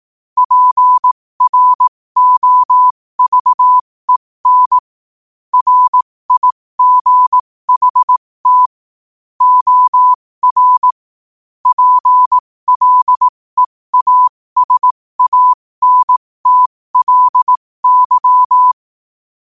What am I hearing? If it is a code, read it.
Morse code, heard as a keyed tone, about 18 wpm, spaced as for 15 wpm, 1000 Hz. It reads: PROVEN RIGHT OR PLEASANTLY
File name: qotdfiles / 15wpm